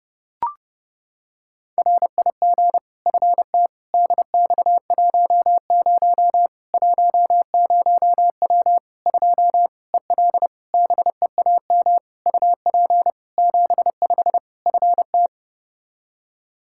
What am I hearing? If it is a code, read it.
Morse code, heard as a keyed tone, about 30 wpm, 700 Hz. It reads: RIG FT DX10 10W 2 EL BEAM UP 75 FT
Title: K1USN SST QSO Practice 30wpm